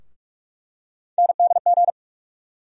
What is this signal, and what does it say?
Morse code, heard as a keyed tone, about 46 wpm, 700 Hz. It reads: NDG
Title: USA Call Sign Postfixes - 3 Letters 45wpm